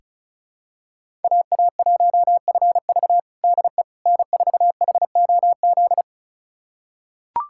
Morse code, heard as a keyed tone, about 36 wpm, 700 Hz. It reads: AA1FV DE N4HOZ
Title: USA Callsign DE Callsign - 5 Characters - 35wpm